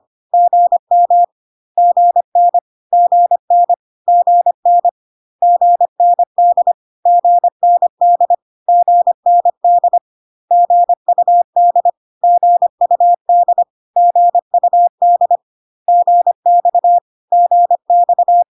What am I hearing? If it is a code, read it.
Morse code, heard as a keyed tone, about 25 wpm, 700 Hz. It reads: GM GN GN GN GND GND GND GUD GUD GUD GX GX
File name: CW abbreviations alphabetical repeated 3 times 25 25wpm